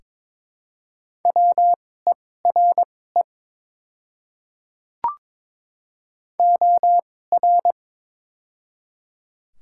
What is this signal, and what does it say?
Morse code, heard as a keyed tone, about 22 wpm, spaced as for 16 wpm, 700 Hz. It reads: WERE OR
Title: 2x Character Spacing - Top 200 Words - 22wpm